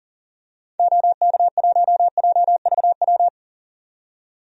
Morse code, heard as a keyed tone, about 40 wpm, 700 Hz. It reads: OK1JVW
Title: DXCC-100 Call Signs - 6 Characters - 40wpm